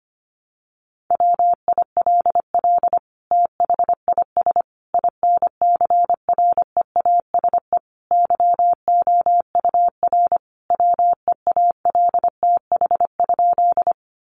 Morse code, heard as a keyed tone, about 25 wpm, 700 Hz. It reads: WILL THIS INCREASE YOUR WEALTH?
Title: Sentences from Top 1200 Words 25wpm